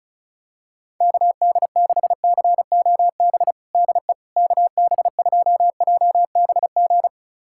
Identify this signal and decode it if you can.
Morse code, heard as a keyed tone, about 34 wpm, 700 Hz. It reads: KD6COB DE KB2JBG
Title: USA Callsign DE Callsign - 6 Characters - 35wpm